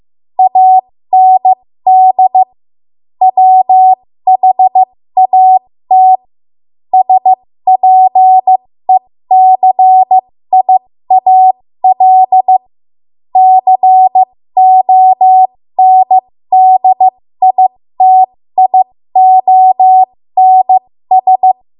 Morse code, heard as a keyed tone, about 15 wpm, spaced as for 13 wpm, 750 Hz. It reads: AND WHAT SPECIAL CONDITIONS
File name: Morse / 13WPM